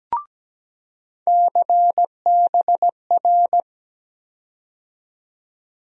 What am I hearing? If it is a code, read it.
Morse code, heard as a keyed tone, about 17 wpm, 700 Hz. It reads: CBR
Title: USA Call Sign Postfixes - 3 Letters 17wpm